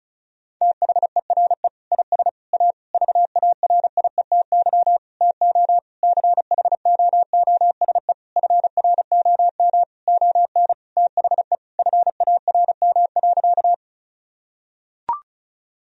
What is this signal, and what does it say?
Morse code, heard as a keyed tone, about 35 wpm, 700 Hz. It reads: THERE IS A VARIETY TO CHOOSE FROM ON THE FARM.
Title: Sentences from Top 1000 Words - Review Part 4 - 35wpm